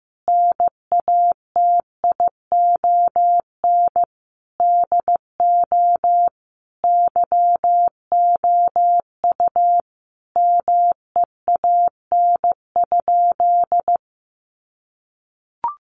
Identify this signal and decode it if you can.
Morse code, heard as a keyed tone, about 15 wpm, 700 Hz. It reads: NATION DO YOU MEAN?
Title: Sentences from Top 900 Words 15wpm